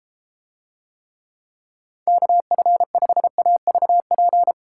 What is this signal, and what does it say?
Morse code, heard as a keyed tone, about 32 wpm, 700 Hz. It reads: KF5AVP